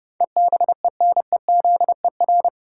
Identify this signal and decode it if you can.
Morse code, heard as a keyed tone, about 32 wpm, 700 Hz. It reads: EBENEZER
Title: Pangram Challenge v2 - 30wpm